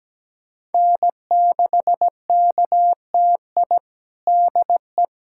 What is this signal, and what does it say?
Morse code, heard as a keyed tone, about 17 wpm, 700 Hz. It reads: N6KTI DE
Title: USA Callsign DE Callsign - 5 Characters - 17wpm